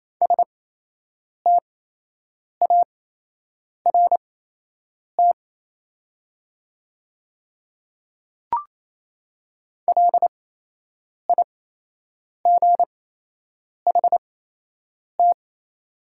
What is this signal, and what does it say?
Morse code, heard as a keyed tone, about 28 wpm, spaced as for 8 wpm, 700 Hz. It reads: START LIGHT
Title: 8x Character Spacing - Top 300 Words - 28wpm